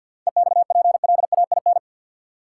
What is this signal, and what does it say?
Morse code, heard as a keyed tone, about 53 wpm, 700 Hz. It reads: EXPLAIN
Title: Sentences from Top 1100 Words - New Word Drill Alt - 50wpm